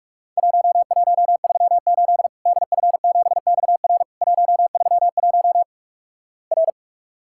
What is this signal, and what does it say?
Morse code, heard as a keyed tone, about 47 wpm, 700 Hz. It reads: 1138 DF7XR 131 R
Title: Worked All Europe DX Contest - QTC Exchange - 45wpm